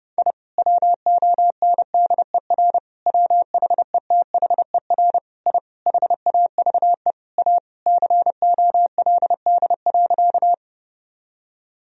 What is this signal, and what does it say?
Morse code, heard as a keyed tone, about 30 wpm, 700 Hz. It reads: I WONDER WHETHER I HAVE A COLD.
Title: Sentences from Top 1000 Words - Review Part 3 - 30wpm